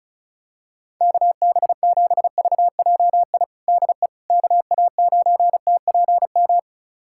Morse code, heard as a keyed tone, about 36 wpm, 700 Hz. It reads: KB7VJI DE KA9TPM
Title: USA Callsign DE Callsign - 6 Characters - 35wpm